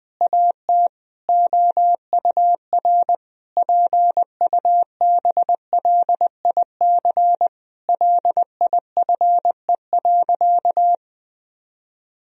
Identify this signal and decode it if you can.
Morse code, heard as a keyed tone, about 20 wpm, 700 Hz. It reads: AT OUR PUBLIC LIFE.
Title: Sentences from Top 1000 Words - Review Part 3 - 20wpm